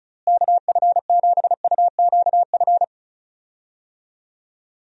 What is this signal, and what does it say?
Morse code, heard as a keyed tone, about 36 wpm, 700 Hz. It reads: KF7UQF
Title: USA Call Signs - 6 Characters - Repeated 2x - 35wpm